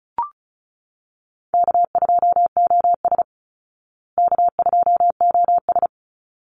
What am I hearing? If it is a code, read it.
Morse code, heard as a keyed tone, about 34 wpm, 700 Hz. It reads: K2OS K2OS
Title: Call Signs Repeated 3x - 35wpm